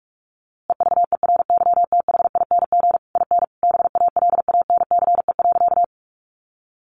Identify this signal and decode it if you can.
Morse code, heard as a keyed tone, about 44 wpm, 700 Hz. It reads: EVERYTHING IN BALANCE.
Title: A Sound Of Thunder 45wpm